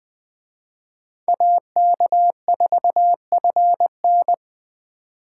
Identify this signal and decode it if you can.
Morse code, heard as a keyed tone, about 24 wpm, 700 Hz. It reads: AK4FN
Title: Speed Racing - USA 2x2 Call Signs 20wpm